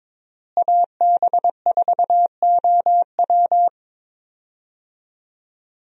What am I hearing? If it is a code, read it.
Morse code, heard as a keyed tone, about 22 wpm, 700 Hz. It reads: AB4OW
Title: USA Call Signs - 5 Characters - Repeated 3x - 22wpm